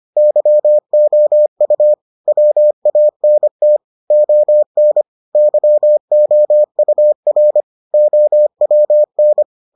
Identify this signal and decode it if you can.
Morse code, heard as a keyed tone, about 25 wpm, 600 Hz. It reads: YOU WANT ON YOUR OWN